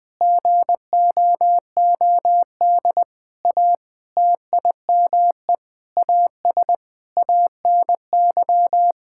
Morse code, heard as a keyed tone, about 21 wpm, 700 Hz. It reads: GOOD A TIME AS ANY
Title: Sentences with 8-Word Phrases - 20wpm